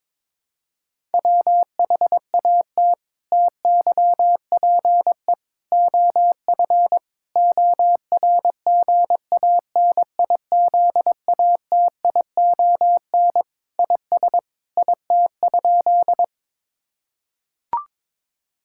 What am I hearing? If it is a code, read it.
Morse code, heard as a keyed tone, about 22 wpm, 700 Hz. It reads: WHAT TYPE OF ORGANIZATION IS IT?
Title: Sentences from Top 1000 Words - Review Part 5 - 22wpm